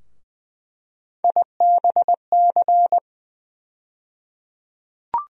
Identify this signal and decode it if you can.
Morse code, heard as a keyed tone, about 20 wpm, 700 Hz. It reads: IBC
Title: USA Call Sign Postfixes - 3 Letters 20wpm